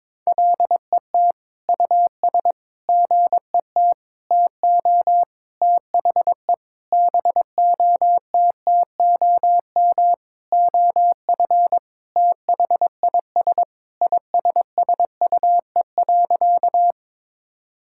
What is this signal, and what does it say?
Morse code, heard as a keyed tone, about 22 wpm, 700 Hz. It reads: LET US GET TO THE BOTTOM OF THIS ISSUE.
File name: Sentences with 7-Word Phrases - 22wpm